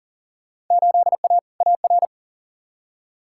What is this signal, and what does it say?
Morse code, heard as a keyed tone, about 39 wpm, 700 Hz. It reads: 8A AR
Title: Field Day Exchanges Encore - 40wpm